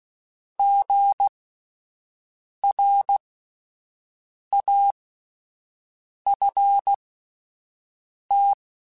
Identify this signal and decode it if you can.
Morse code, heard as a keyed tone, about 16 wpm, spaced as for 5 wpm, 800 Hz. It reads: GRAFT
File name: Spaced Alphas / Spaced Alphas mp3s / Gs sp @10